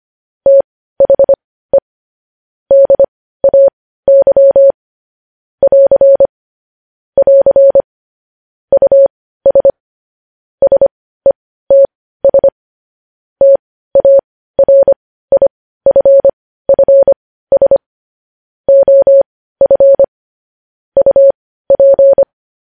Morse code, heard as a keyed tone, about 25 wpm, spaced as for 15 wpm, 550 Hz. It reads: THE DAY <AR> <AR> US SETS TARIFFS OF UP
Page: News headlines in morse code at 15 wpm